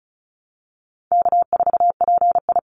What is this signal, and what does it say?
Morse code, heard as a keyed tone, about 34 wpm, 700 Hz. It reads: K4PI
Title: Easy Call Signs Repeated 3x - 35wpm